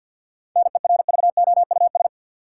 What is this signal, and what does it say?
Morse code, heard as a keyed tone, about 51 wpm, 700 Hz. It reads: NERVOUS
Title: Seven Letter Words - 50wpm